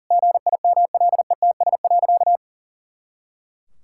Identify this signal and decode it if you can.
Morse code, heard as a keyed tone, about 40 wpm, 700 Hz. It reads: GIMLETS.
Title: Pangram Challenge v2 - 40wpm